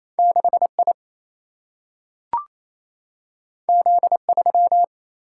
Morse code, heard as a keyed tone, about 28 wpm, 700 Hz. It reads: 6 I Z 3
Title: Single-Letter Single-Number 28wpm